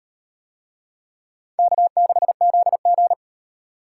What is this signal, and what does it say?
Morse code, heard as a keyed tone, about 35 wpm, 700 Hz. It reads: K6ZG